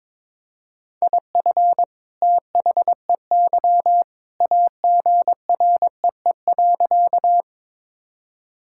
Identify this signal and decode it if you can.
Morse code, heard as a keyed tone, about 22 wpm, 700 Hz. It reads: IF THEY AGREE.
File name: practice / 22wpm / Sentences with 2-Word Phrases - 22wpm